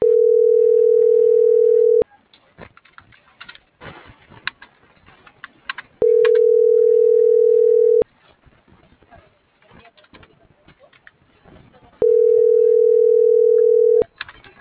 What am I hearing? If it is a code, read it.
Ringback tone (North America), 440 + 480 Hz.